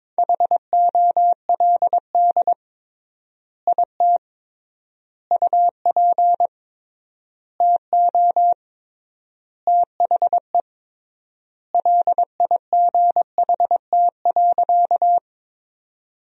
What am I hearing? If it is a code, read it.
Morse code, heard as a keyed tone, about 22 wpm, 700 Hz. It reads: HOLD IT UP TO THE LIGHT.
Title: 3x Word-Spacing - Sentences from Top 400 Words - 22wpm